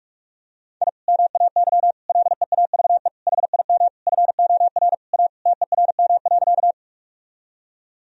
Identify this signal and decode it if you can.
Morse code, heard as a keyed tone, about 46 wpm, 700 Hz. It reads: I MAY LEAVE HIM FOR A TERM.
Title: Sentences from Top 1000 Words - Review Part 4 - 45wpm